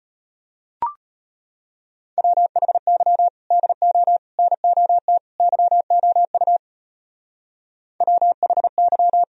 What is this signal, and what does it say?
Morse code, heard as a keyed tone, about 32 wpm, 700 Hz. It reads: WHY DO NOT YOU WHY